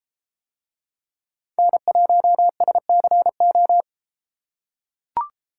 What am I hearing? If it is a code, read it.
Morse code, heard as a keyed tone, about 33 wpm, 700 Hz. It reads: N1SCO